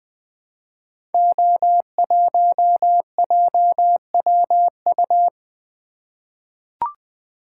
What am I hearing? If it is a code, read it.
Morse code, heard as a keyed tone, about 20 wpm, 700 Hz. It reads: O1JWU
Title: Code Groups - 5 Letter:Numbers - 20wpm